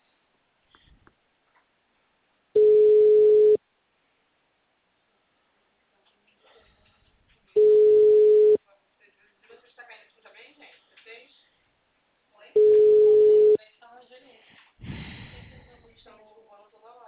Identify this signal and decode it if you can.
Ringback tone (Europe, 425 Hz), with speech.
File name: out- -2001- - - .21341